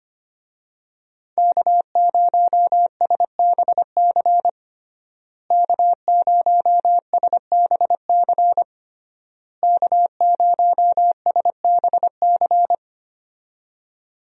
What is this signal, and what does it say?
Morse code, heard as a keyed tone, about 25 wpm, 700 Hz. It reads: K0SBC K0SBC K0SBC
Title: USA Call Signs - 5 Characters - Repeated 3x - 25wpm